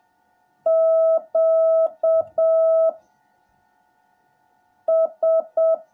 Morse code, heard as a keyed tone, about 7 wpm, 650 Hz. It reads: Q S
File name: jh8cbh / 2021cw / cw17a